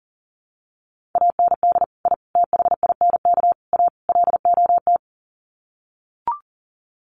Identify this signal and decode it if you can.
Morse code, heard as a keyed tone, about 40 wpm, 700 Hz. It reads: AND I THINK A LOT
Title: Sets of 5 Words Encore2 40wpm